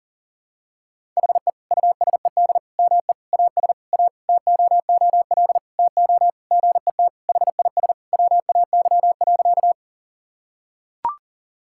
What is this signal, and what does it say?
Morse code, heard as a keyed tone, about 40 wpm, 700 Hz. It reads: HE USED ME AS A TOOL TO GET HIS WAY.
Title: Sentences from Top 1000 Words - Review Part 6 - 40wpm